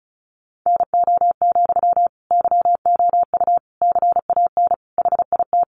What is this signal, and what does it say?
Morse code, heard as a keyed tone, about 34 wpm, 700 Hz. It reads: NO, YOU CAN HIT
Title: Sentences from Top 1200 Words 35wpm